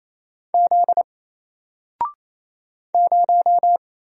Morse code, heard as a keyed tone, about 27 wpm, 700 Hz. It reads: Z 0
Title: Single Letter-Number Alt 28wpm